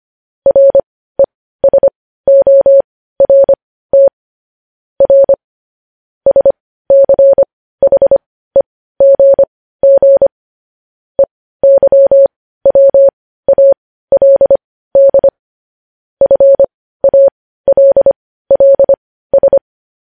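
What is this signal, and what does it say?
Morse code, heard as a keyed tone, about 25 wpm, spaced as for 15 wpm, 550 Hz. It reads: RESORT R SCHEGG EYWALD FALLS
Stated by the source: Morse news headlines practice sent at 15 WPM.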